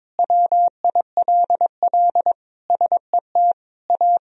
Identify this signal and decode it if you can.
Morse code, heard as a keyed tone, about 22 wpm, 700 Hz. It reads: WILL SET A
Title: Sentences from Top 1000 Words - Review Part 1 - 22wpm